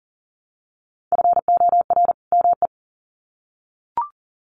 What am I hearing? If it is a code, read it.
Morse code, heard as a keyed tone, about 40 wpm, 700 Hz. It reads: FOR ME
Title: Sets of 2 Words 40wpm